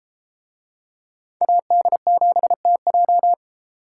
Morse code, heard as a keyed tone, about 31 wpm, 700 Hz. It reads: AD7TJ